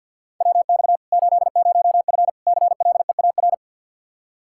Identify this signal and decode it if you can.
Morse code, heard as a keyed tone, about 52 wpm, 700 Hz. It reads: WX 80F CLEAR
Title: Practice QSO - Exchange 2 - Info - 50wpm